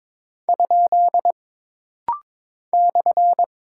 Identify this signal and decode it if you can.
Morse code, heard as a keyed tone, about 22 wpm, 700 Hz. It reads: ? /
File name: practice / 22wpm / Single Punctuation-Prosign - Rapid-Fire - 22wpm